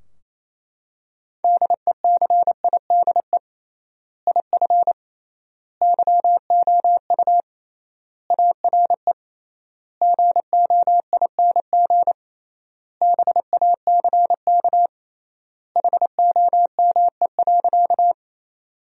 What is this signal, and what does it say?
Morse code, heard as a keyed tone, about 28 wpm, 700 Hz. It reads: DECIDE IF YOU ARE GOING BACK HOME.
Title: 3x Word-Spacing - Sentences from Top 500 Words - 28wpm